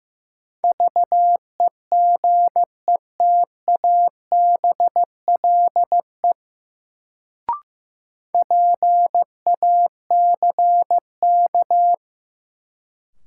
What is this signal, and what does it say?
Morse code, heard as a keyed tone, about 15 wpm, 700 Hz. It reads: VEGETABLE PACK
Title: Sentences from Top 1300 Words - New Word Drill Alt - 15wpm